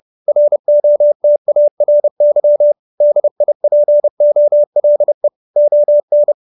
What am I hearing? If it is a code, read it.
Morse code, heard as a keyed tone, about 30 wpm, 590 Hz. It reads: ROTARY DIPOLE ON